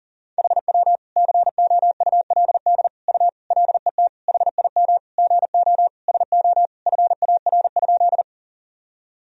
Morse code, heard as a keyed tone, about 40 wpm, 700 Hz. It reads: HW COULD U LET HIM GO SO FAR?
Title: Sentences from Top 300 Words and Abbreviations 40wpm